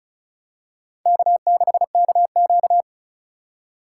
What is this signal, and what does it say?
Morse code, heard as a keyed tone, about 36 wpm, 700 Hz. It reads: K6KQ
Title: USA Call Signs - 4 Characters - 4x Character Spacing - 35wpm